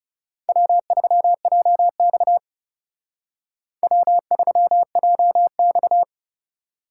Morse code, heard as a keyed tone, about 34 wpm, 700 Hz. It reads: W3JX W3JX
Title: Speed Racing - USA 1x2 Call Signs 30wpm